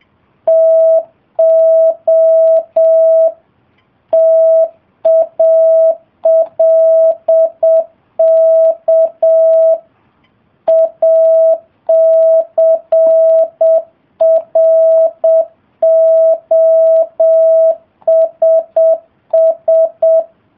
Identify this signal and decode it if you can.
Morse code, heard as a keyed tone, about 7 wpm, 650 Hz. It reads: TO TALK ACROSS